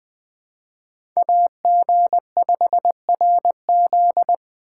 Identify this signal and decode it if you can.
Morse code, heard as a keyed tone, about 20 wpm, 700 Hz. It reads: AG5RZ
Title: Speed Racing - USA 2x2 Call Signs 17wpm